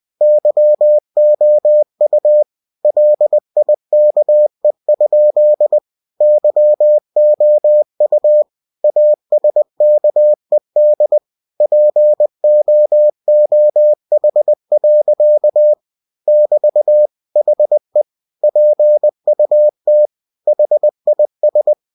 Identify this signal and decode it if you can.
Morse code, heard as a keyed tone, about 20 wpm, 600 Hz. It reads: YOU LIKE? YOU ASKED POOH. <BT> HE PUT HIS
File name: cw-books / winnie-the-pooh / winnie-20wpm-page-0005